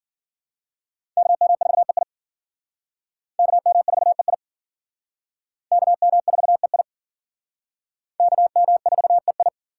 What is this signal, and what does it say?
Morse code, heard as a keyed tone, about 50 wpm, 700 Hz. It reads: KM4EI KM4EI KM4EI KM4EI
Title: Speed Racing - USA 2x2 Call Signs 40wpm